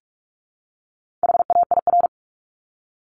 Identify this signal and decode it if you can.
Morse code, heard as a keyed tone, about 44 wpm, 700 Hz. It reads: HAIR
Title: Top 1751-2000 Words 45wpm